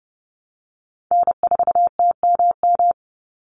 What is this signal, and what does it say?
Morse code, heard as a keyed tone, about 30 wpm, 700 Hz. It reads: N4TMM
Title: Call Signs Repeated 3x - 30wpm